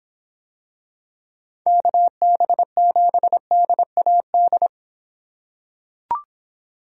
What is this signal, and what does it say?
Morse code, heard as a keyed tone, about 26 wpm, 700 Hz. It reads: KB7DAD